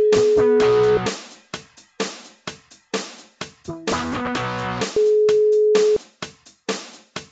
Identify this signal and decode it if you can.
Ringback tone (Europe, 425 Hz).